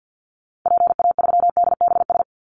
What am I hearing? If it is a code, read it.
Morse code, heard as a keyed tone, about 50 wpm, 700 Hz. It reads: PA3DBS
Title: Call Signs - 50wpm